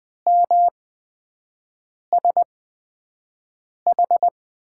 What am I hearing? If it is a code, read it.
Morse code, heard as a keyed tone, about 20 wpm, spaced as for 5 wpm, 700 Hz. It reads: MSH
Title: USA Call Signs - 5 Characters - 8x Character Spacing - 20wpm